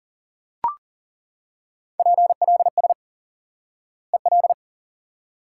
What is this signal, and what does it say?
Morse code, heard as a keyed tone, about 40 wpm, 700 Hz. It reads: PLS EL
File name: IQE Course 3 - Lesson 89 - YAGI - Two Elements - 40wpm